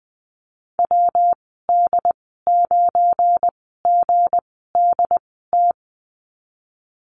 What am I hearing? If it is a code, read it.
Morse code, heard as a keyed tone, about 20 wpm, spaced as for 15 wpm, 700 Hz. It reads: WD9GDT
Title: 173 Lesson 28g - Number 9 - Call Signs 20wpm